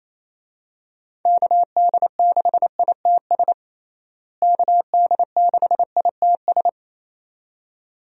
Morse code, heard as a keyed tone, about 28 wpm, 700 Hz. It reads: KD6ITS KD6ITS
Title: USA Call Signs - 6 Characters - Repeated 2x - 28wpm